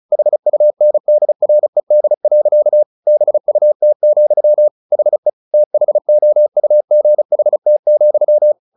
Morse code, heard as a keyed tone, about 34 wpm, 600 Hz. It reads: HUNDRED. BUT, HE THOUGHT,
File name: the old man and the sea / the old man and the sea 1 35wpm